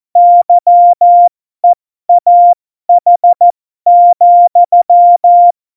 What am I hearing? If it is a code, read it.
Morse code, heard as a keyed tone, about 14 wpm, spaced as for 12 wpm, 700 Hz. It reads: YEAH,
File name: Morsekurs / Witze